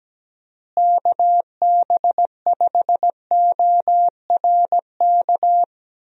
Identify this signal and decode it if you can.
Morse code, heard as a keyed tone, about 17 wpm, 700 Hz. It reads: KB5ORK
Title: USA Call Signs - 6 Characters - Repeated 2x - 17wpm